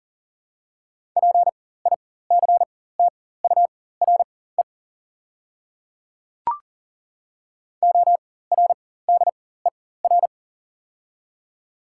Morse code, heard as a keyed tone, about 40 wpm, spaced as for 19 wpm, 700 Hz. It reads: PICTURE ORDER
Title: 4x Character Spacing - Top 500 Words - 40wpm